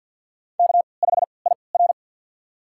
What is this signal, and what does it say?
Morse code, heard as a keyed tone, about 52 wpm, spaced as for 29 wpm, 700 Hz. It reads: K5IR
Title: USA Call Signs - 4 Characters - 3x Character Spacing - 50wpm